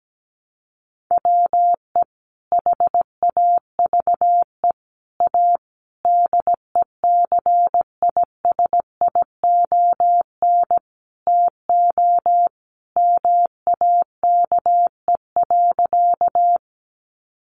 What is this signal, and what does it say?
Morse code, heard as a keyed tone, about 17 wpm, 700 Hz. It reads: WE HAVE A DECISION TO MAKE.
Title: Sentences from Top 1000 Words 17wpm